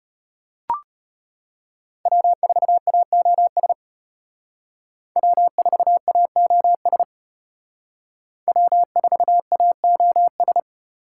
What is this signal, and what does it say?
Morse code, heard as a keyed tone, about 31 wpm, 700 Hz. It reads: W4AOS W4AOS W4AOS
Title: Speed Racing - USA 1x3 Call Signs 25wpm